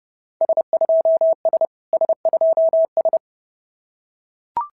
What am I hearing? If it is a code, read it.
Morse code, heard as a keyed tone, about 30 wpm, 690 Hz. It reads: S2S S2S
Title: SOTA QSO Practice 30wpm